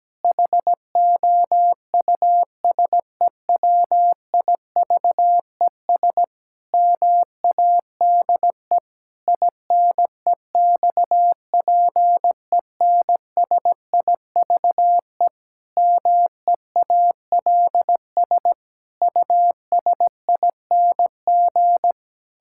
Morse code, heard as a keyed tone, about 17 wpm, 700 Hz. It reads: HOUSEWIVES MADE INEXPENSIVE MEALS USING
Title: Pangram Challenge v2 - 17wpm